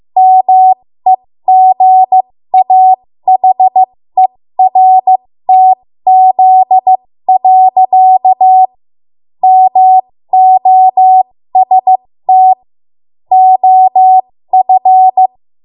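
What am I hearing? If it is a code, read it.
Morse code, heard as a keyed tone, about 15 wpm, spaced as for 13 wpm, 750 Hz. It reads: MEGAHERTZ. MOST OF
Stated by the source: 13 wpm Morse code practice archive transmission.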